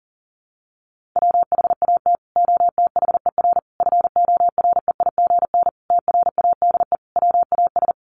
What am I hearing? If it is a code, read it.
Morse code, heard as a keyed tone, about 40 wpm, 700 Hz. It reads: WHAT OTHER FOREIGN TRADE WAS
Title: Sentences from Top 1400 Words 40wpm